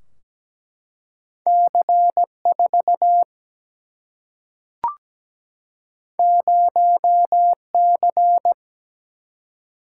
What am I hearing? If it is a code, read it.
Morse code, heard as a keyed tone, about 17 wpm, 700 Hz. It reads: C4 0C
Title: Single-Letter Single-Number 17wpm